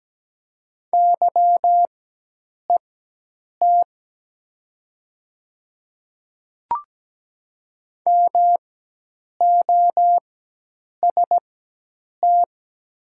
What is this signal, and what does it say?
Morse code, heard as a keyed tone, about 17 wpm, spaced as for 8 wpm, 700 Hz. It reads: YET MOST
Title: 4x Character Spacing - Top 500 Words - 17wpm